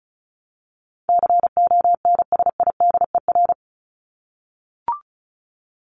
Morse code, heard as a keyed tone, about 34 wpm, 700 Hz. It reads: CONSIDER
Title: ED Suffix - Top 100 Words with Base Words 35wpm